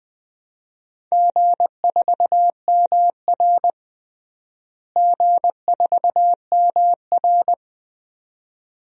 Morse code, heard as a keyed tone, about 20 wpm, 700 Hz. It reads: G4MR G4MR
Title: DXCC-100 Call Signs - 4 Characters - Repeated 2x - 20wpm